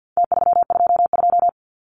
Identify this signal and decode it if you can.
Morse code, heard as a keyed tone, about 50 wpm, 700 Hz. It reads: T322
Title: Numbers - Contest Style - 4-Digit 50wpm